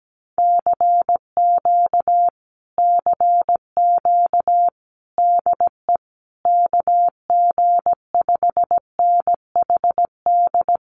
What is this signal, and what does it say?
Morse code, heard as a keyed tone, about 17 wpm, 700 Hz. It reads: CQ CQ DE KG5NHD
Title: Common QSO Phrases Encore 17wpm